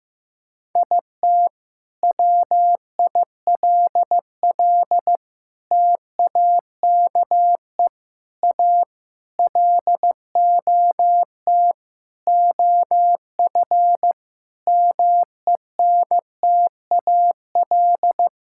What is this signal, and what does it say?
Morse code, heard as a keyed tone, about 14 wpm, 700 Hz. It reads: IT WILL TAKE A LOT OF MENTAL
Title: Sentences from Top 1000 Words - Review Part 6 - 15wpm